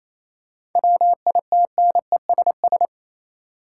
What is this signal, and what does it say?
Morse code, heard as a keyed tone, about 28 wpm, 700 Hz. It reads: WITNESS
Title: Seven Letter Words - 28wpm